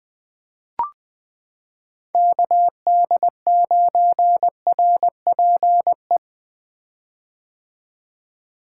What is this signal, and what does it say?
Morse code, heard as a keyed tone, about 20 wpm, 700 Hz. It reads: KD9RPE
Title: 2023 POTA Hunter Call Signs - 20wpm